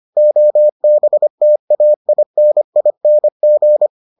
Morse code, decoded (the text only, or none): OBTAINING